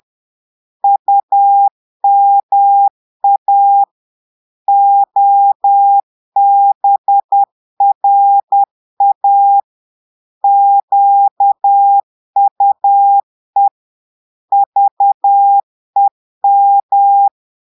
Morse code, heard as a keyed tone, about 10 wpm, 800 Hz. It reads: UMA OBRA QUE VEM
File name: Portugais - texte en clair 1 - vitesse - 10 mots minute